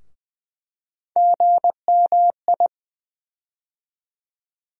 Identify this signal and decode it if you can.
Morse code, heard as a keyed tone, about 20 wpm, 700 Hz. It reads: GMI
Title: USA Call Sign Postfixes - 3 Letters 20wpm